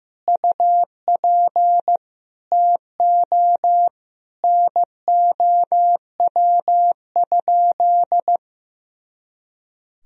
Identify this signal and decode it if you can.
Morse code, heard as a keyed tone, about 15 wpm, 700 Hz. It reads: UP TO NOW?
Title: Sentences from Top 100 Words and Abbreviations 15wpm